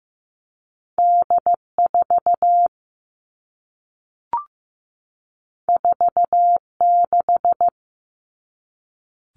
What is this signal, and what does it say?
Morse code, heard as a keyed tone, about 15 wpm, 700 Hz. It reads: D4 46
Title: All Cut Numbers - 2-Digit 15wpm